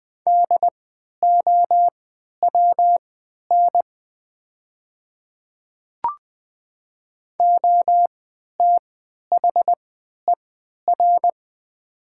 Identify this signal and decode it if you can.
Morse code, heard as a keyed tone, about 20 wpm, spaced as for 11 wpm, 700 Hz. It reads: DOWN OTHER
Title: 3x Character Spacing - Top 100 Words - 20wpm